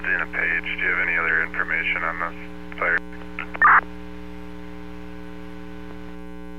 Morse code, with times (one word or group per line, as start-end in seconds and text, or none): none